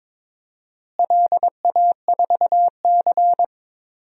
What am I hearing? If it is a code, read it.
Morse code, heard as a keyed tone, about 22 wpm, 700 Hz. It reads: LA4C
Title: DXCC-100 Call Signs - 4 Characters - Repeated 2x - 22wpm